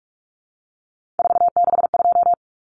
Morse code, heard as a keyed tone, about 44 wpm, 700 Hz. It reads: 462
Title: Common Cut Numbers - 3-Digit 45wpm